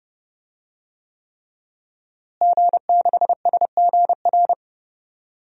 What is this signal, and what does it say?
Morse code, heard as a keyed tone, about 30 wpm, 700 Hz. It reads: G6SGR
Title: DXCC-100 Call Signs - 5 Characters - Repeated 3x - 30wpm